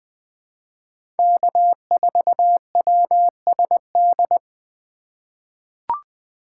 Morse code, heard as a keyed tone, about 21 wpm, 700 Hz. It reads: K4WSD
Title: 2023 POTA Activator Call Signs - 20wpm